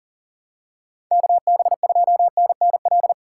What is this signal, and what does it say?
Morse code, heard as a keyed tone, about 37 wpm, 700 Hz. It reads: KB2NNL
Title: USA Call Signs - 6 Characters - 6x Character Spacing - 40wpm